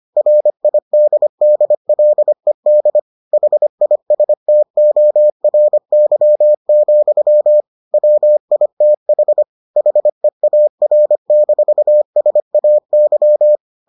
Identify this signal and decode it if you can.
Morse code, heard as a keyed tone, about 25 wpm, 600 Hz. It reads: RIDDLED HISTORY, WITH HEAR-SAY